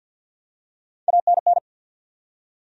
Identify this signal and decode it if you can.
Morse code, heard as a keyed tone, about 53 wpm, 700 Hz. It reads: ANN
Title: USA Common Names - Max 3 Letters - 50wpm